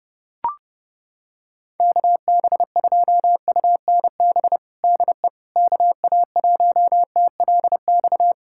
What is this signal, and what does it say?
Morse code, heard as a keyed tone, about 32 wpm, 700 Hz. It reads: KB2UNB DE KA1TLX
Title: USA Callsign DE Callsign - 6 Characters - 30wpm